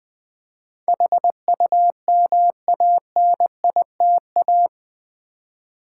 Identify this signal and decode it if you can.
Morse code, heard as a keyed tone, about 20 wpm, 700 Hz. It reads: HUMANITA
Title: Build A Word - 4 Letters At A Time - 12-Letter Words - 20wpm